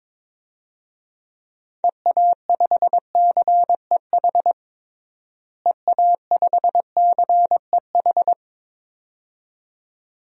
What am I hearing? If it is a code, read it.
Morse code, heard as a keyed tone, about 22 wpm, 700 Hz. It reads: EA5CEH EA5CEH
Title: DXCC-100 Call Signs - 6 Characters - Repeated 2x - 22wpm